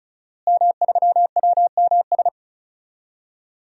Morse code, heard as a keyed tone, about 36 wpm, 700 Hz. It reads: M3WMS
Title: DXCC-100 Call Signs - 5 Characters - Repeated 3x - 35wpm